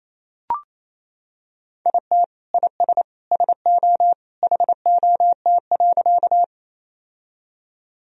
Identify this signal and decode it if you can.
Morse code, heard as a keyed tone, about 28 wpm, 700 Hz. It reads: IT IS SO HOT.
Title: Sentences from Top 1000 Words - Review Part 3 - 28wpm